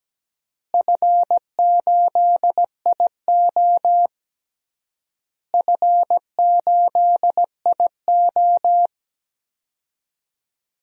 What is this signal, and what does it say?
Morse code, heard as a keyed tone, about 17 wpm, 700 Hz. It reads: F8IO F8IO
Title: DXCC-100 Call Signs - 4 Characters - Repeated 3x - 17wpm